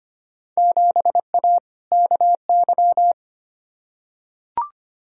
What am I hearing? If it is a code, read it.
Morse code, heard as a keyed tone, about 25 wpm, 700 Hz. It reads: 7A KY
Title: Field Day Exchanges Encore - 25wpm